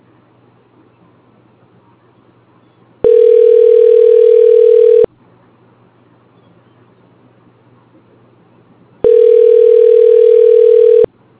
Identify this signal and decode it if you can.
Ringback tone (North America), 440 + 480 Hz.